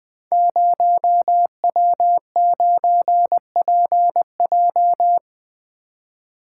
Morse code, heard as a keyed tone, about 20 wpm, 700 Hz. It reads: 0W9PJ
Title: Code Groups - 5 Letter:Numbers - 20wpm